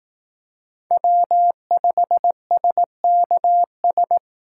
Morse code, heard as a keyed tone, about 18 wpm, 700 Hz. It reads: W5SKS
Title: Speed Racing - USA Call Signs 15wpm